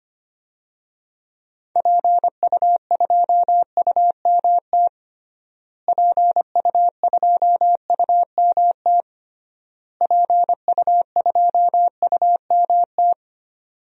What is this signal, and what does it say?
Morse code, heard as a keyed tone, about 25 wpm, 700 Hz. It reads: PU2UMT PU2UMT PU2UMT
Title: DXCC-100 Call Signs - 6 Characters - Repeated 3x - 25wpm